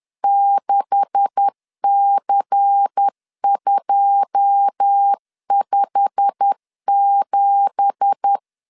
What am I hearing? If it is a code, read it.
Morse code, heard as a keyed tone, about 10 wpm, 800 Hz. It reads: 6C257
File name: CW / DET / 1 9 1500C S B 10wpm 4439